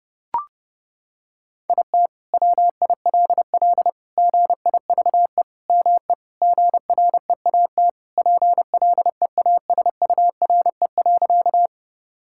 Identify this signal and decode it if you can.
Morse code, heard as a keyed tone, about 32 wpm, 700 Hz. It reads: IT WILL GIVE ME GREAT PLEASURE.
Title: Sentences from Top 1600 Words 30wpm